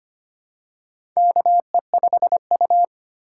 Morse code, heard as a keyed tone, about 25 wpm, 700 Hz. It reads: KE5U
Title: USA Call Signs - 4 Characters - 3x Character Spacing - 25wpm